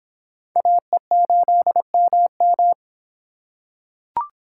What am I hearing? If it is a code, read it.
Morse code, heard as a keyed tone, about 26 wpm, 700 Hz. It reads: AE8MM